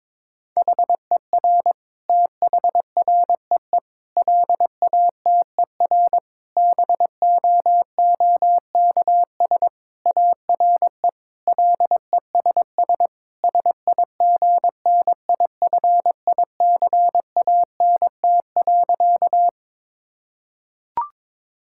Morse code, heard as a keyed tone, about 22 wpm, 700 Hz. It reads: HER THREE LATER BOOKS ARE LESS SIGNIFICANT.
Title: Sentences from Top 1000 Words - Review Part 6 - 22wpm